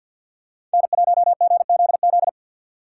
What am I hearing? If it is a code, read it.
Morse code, heard as a keyed tone, about 52 wpm, 700 Hz. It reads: N1GZZ
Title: USA Call Signs - 5 Characters - 2x Character Spacing - 50wpm